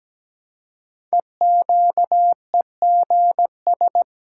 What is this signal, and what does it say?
Morse code, heard as a keyed tone, about 17 wpm, 700 Hz. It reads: EQEGS
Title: Code Groups - 5 Letters - 17wpm